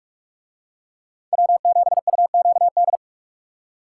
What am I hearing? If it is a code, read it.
Morse code, heard as a keyed tone, about 40 wpm, 700 Hz. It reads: W7UQD